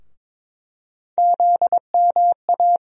Morse code, heard as a keyed tone, about 23 wpm, 700 Hz. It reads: ZMA
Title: USA Call Sign Postfixes - 3 Letters 22wpm